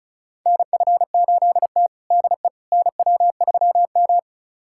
Morse code, heard as a keyed tone, about 34 wpm, 700 Hz. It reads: NF8T DE NW3M
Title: USA Callsign DE Callsign - 4 Characters - 35wpm